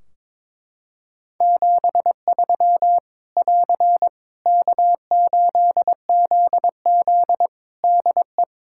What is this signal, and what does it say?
Morse code, heard as a keyed tone, about 22 wpm, 700 Hz. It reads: 73 <AR> K8ZZ DE
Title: K1USN SST QSO Practice 22wpm